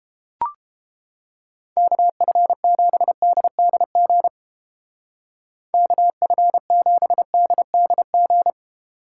Morse code, heard as a keyed tone, about 29 wpm, 700 Hz. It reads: KF7DDG KF7DDG